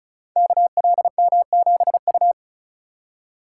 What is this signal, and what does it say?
Morse code, heard as a keyed tone, about 36 wpm, 690 Hz. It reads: KLM7U
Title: Code Groups - 5 Letter:Numbers - 35wpm